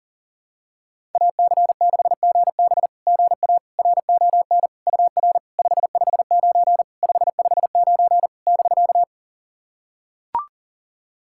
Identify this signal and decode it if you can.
Morse code, heard as a keyed tone, about 40 wpm, 700 Hz. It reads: AC6GB GA RON UR 559 559 <BK>
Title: SOTA QSO Practice 40wpm